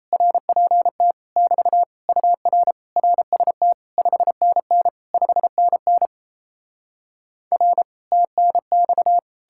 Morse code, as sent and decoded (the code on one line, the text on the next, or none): .-. .--. - / -...- / ..- .-. / .-. ... - / ..... -. -. / ..... -. -. / .-. / - -. -..-
RPT <BT> UR RST 5NN 5NN R TNX